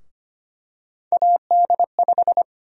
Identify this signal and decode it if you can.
Morse code, heard as a keyed tone, about 25 wpm, 700 Hz. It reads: AD5
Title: USA Call Sign Prefixes - Repeated 2x - 25wpm